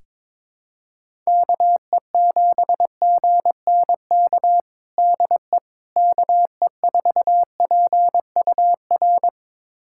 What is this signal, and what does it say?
Morse code, heard as a keyed tone, about 22 wpm, 700 Hz. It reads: KE7GNK DE KE4PUR
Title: USA Callsign DE Callsign - 6 Characters - 22wpm